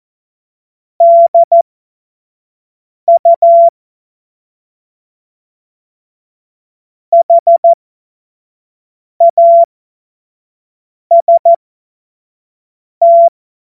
Morse code, heard as a keyed tone, about 14 wpm, spaced as for 5 wpm, 680 Hz. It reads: DU HAST
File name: Morsekurs / Witze / Witz 21. Februar 2023 25 F70